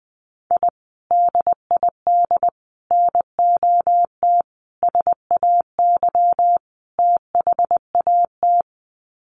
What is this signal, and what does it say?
Morse code, heard as a keyed tone, about 20 wpm, 700 Hz. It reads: I DID NOT SAY THAT
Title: Sets of 5 Words Encore2 20wpm